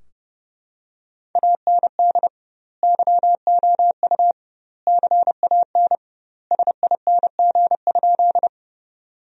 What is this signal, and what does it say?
Morse code, heard as a keyed tone, about 32 wpm, 700 Hz. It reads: AND YOU CAN SING?
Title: 2x Word-Spacing - Sentences from Top 400 Words - 30wpm